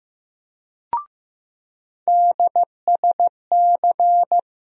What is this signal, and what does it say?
Morse code, heard as a keyed tone, about 16 wpm, 700 Hz. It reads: DSC
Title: USA Call Sign Postfixes - 3 Letters 15wpm